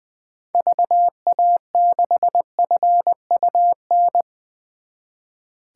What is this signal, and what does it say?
Morse code, heard as a keyed tone, about 21 wpm, 700 Hz. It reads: VA6FUN
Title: 2023 SOTA Chaser Call Signs - 20wpm